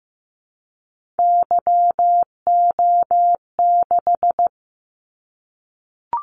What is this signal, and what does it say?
Morse code, heard as a keyed tone, about 15 wpm, 700 Hz. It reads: YO6
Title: Callsign Prefixes for Top 25 DXCC - 15wpm